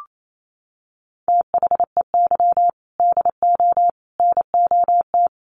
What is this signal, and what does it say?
Morse code, heard as a keyed tone, about 28 wpm, 700 Hz. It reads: THEY DO NOT
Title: Sets of 4 Words 28wpm